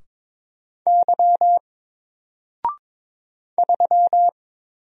Morse code, heard as a keyed tone, about 22 wpm, 700 Hz. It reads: Y 3
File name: Single-Letter-Number-22wpm